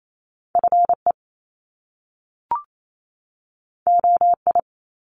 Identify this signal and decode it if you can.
Morse code, heard as a keyed tone, about 28 wpm, 700 Hz. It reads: FE OI
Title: Top Two Letter Combinations in Words 28wpm